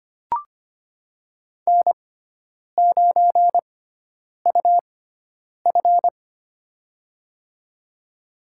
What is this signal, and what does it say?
Morse code, heard as a keyed tone, about 25 wpm, spaced as for 9 wpm, 700 Hz. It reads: N9UF
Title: USA Call Signs - 6x Character Spacing - 25wpm